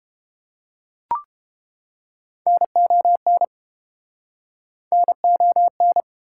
Morse code, heard as a keyed tone, about 30 wpm, 700 Hz. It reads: NON NON